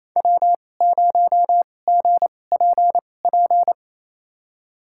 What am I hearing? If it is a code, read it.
Morse code, heard as a keyed tone, about 28 wpm, spaced as for 20 wpm, 700 Hz. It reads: W0GPP
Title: USA Call Signs - 5 Characters - 2x Character Spacing - 28wpm